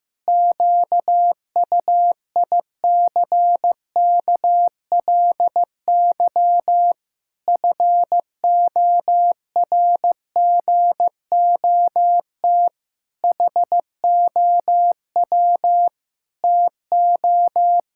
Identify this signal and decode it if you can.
Morse code, heard as a keyed tone, about 15 wpm, 700 Hz. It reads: QUICKLY FORGOT HOW TO
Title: Pangram Challenge v2 - 15wpm